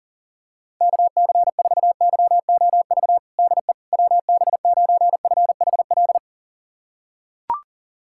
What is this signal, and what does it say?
Morse code, heard as a keyed tone, about 40 wpm, 700 Hz. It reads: KC4YOV DE WB9FHL
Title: USA Callsign DE Callsign - 6 Characters - 40wpm